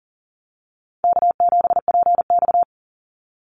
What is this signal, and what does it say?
Morse code, heard as a keyed tone, about 39 wpm, 700 Hz. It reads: K7PX
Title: NA SOTA Activator Callsigns 40wpm